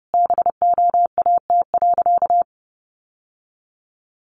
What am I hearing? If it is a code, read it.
Morse code, heard as a keyed tone, about 28 wpm, 700 Hz. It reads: BOAT.
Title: Sentences from Top 1400 Words 30wpm